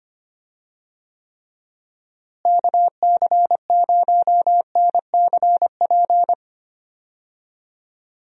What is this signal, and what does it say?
Morse code, heard as a keyed tone, about 25 wpm, 700 Hz. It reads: KC0NCP
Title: USA Call Signs - 6 Characters - Repeated 3x - 25wpm